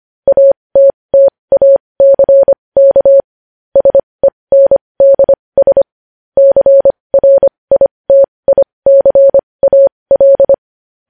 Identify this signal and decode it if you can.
Morse code, heard as a keyed tone, about 25 wpm, spaced as for 20 wpm, 550 Hz. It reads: ATTACK SENDS CRITICAL
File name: podcast / News headlines-1x-25x20